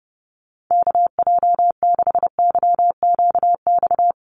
Morse code, heard as a keyed tone, about 30 wpm, 700 Hz. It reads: KJ6YQX
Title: USA Call Signs Repeated 3x 30wpm